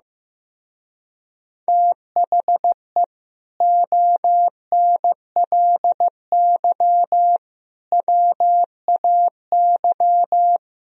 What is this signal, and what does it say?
Morse code, heard as a keyed tone, about 16 wpm, 700 Hz. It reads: THE ONLY WAY
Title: Speed-Racing - Sets of 4 Words 15wpm